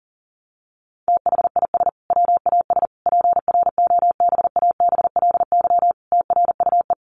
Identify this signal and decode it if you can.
Morse code, heard as a keyed tone, about 40 wpm, 700 Hz. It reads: THIS WAS PROBABLY TRUE
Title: Sentences from Top 700 Words 40wpm